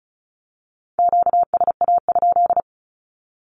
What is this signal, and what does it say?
Morse code, heard as a keyed tone, about 36 wpm, 700 Hz. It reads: QSA?
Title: Q-Codes with Spoken Meaning - 35wpm